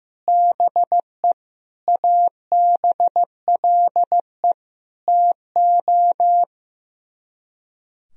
Morse code, heard as a keyed tone, about 15 wpm, 700 Hz. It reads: BE ABLE TO
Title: Speed-Racing - Sets of 5 Words 15wpm